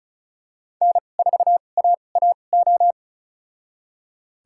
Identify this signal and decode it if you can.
Morse code, heard as a keyed tone, about 34 wpm, spaced as for 25 wpm, 700 Hz. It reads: N4AAO
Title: USA Call Signs - 2x Character Spacing - 35wpm